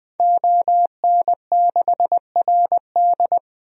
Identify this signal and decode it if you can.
Morse code, heard as a keyed tone, about 20 wpm, 700 Hz. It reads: ON6RD
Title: DXCC-100 Call Signs - 5 Characters - 20wpm